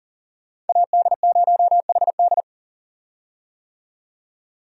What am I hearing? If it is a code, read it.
Morse code, heard as a keyed tone, about 37 wpm, 700 Hz. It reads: AD0HD
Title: USA Call Signs - 5 Characters - 40wpm